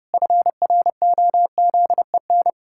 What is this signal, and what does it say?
Morse code, heard as a keyed tone, about 30 wpm, 700 Hz. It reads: FROZEN
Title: Pangram Challenge v2 - 30wpm